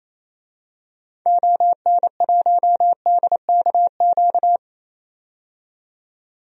Morse code, heard as a keyed tone, about 28 wpm, 700 Hz. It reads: ON1DKQ
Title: DXCC-100 Call Signs - 6 Characters - 28wpm